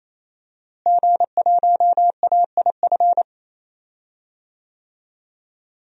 Morse code, heard as a keyed tone, about 28 wpm, 700 Hz. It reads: G1AIF
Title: DXCC-100 Call Signs - 5 Characters - Repeated 3x - 28wpm